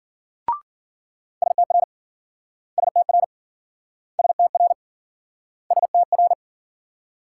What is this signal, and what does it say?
Morse code, heard as a keyed tone, about 53 wpm, 700 Hz. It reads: STR STR STR STR